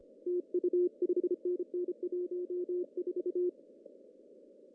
Morse code, heard as a keyed tone, about 25 wpm, 340 Hz. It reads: TU 5NN 14
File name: CW / cqww-cw-k5tr